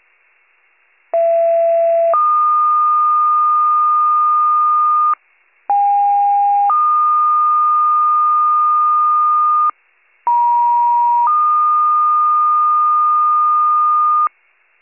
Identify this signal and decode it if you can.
Two-tone page, heard as a one-second tone, then a three-second tone, about 670 then 1220 Hz, about 810 then 1220 Hz, about 950 then 1220 Hz. It is heard from 1.1 to 14.3 s.